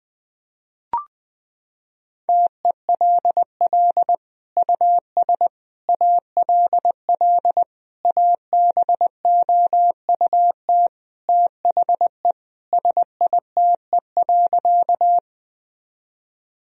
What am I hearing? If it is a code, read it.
Morse code, heard as a keyed tone, about 21 wpm, 700 Hz. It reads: TELL US ALL ABOUT THE SITE.
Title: Sentences from Top 1000 Words - Review Part 4 - 20wpm